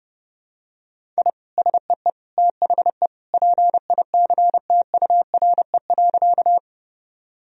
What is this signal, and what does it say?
Morse code, heard as a keyed tone, about 30 wpm, 700 Hz. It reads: I SEE THE PICTURE.
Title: Sentences from Top 1000 Words - Review Part 1 - 30wpm